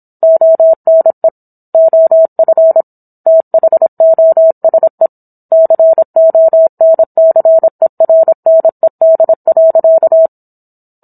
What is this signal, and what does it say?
Morse code, heard as a keyed tone, about 27 wpm, 650 Hz. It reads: ONE OF THOSE CONCERNED.
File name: Book One / 17-Chapter 17 - 26 WPM